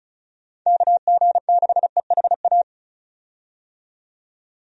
Morse code, heard as a keyed tone, about 36 wpm, 700 Hz. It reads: KG6EHA
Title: USA Call Signs - 6 Characters - 35wpm